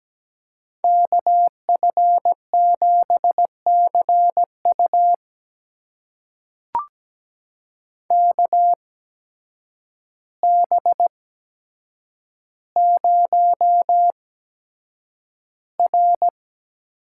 Morse code, heard as a keyed tone, about 17 wpm, spaced as for 8 wpm, 700 Hz. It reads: KF7CU KB0R
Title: USA Call Signs - 5 Characters - 8x Character Spacing - 17wpm